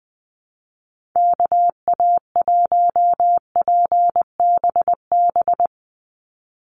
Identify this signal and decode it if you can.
Morse code, heard as a keyed tone, about 20 wpm, 700 Hz. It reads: KA1PBB
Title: USA Call Signs 20wpm